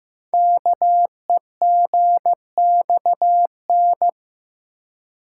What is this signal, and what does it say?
Morse code, heard as a keyed tone, about 15 wpm, 690 Hz. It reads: KEGXN
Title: Code Groups - 5 Letters - 15wpm